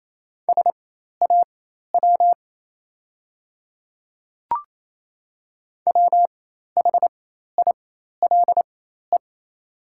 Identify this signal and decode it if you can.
Morse code, heard as a keyed tone, about 27 wpm, spaced as for 13 wpm, 700 Hz. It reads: SAW WHILE
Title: 4x Character Spacing - Top 300 Words - 28wpm